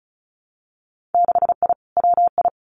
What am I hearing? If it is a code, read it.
Morse code, heard as a keyed tone, about 36 wpm, 700 Hz. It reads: 6I WI
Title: Winter Field Day Exchanges Encore 35wpm